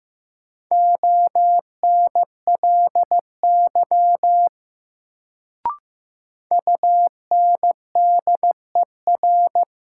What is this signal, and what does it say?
Morse code, heard as a keyed tone, about 15 wpm, 700 Hz. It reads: ONLY UNDER
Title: Sets of 2 Words - New Word Drill - 15wpm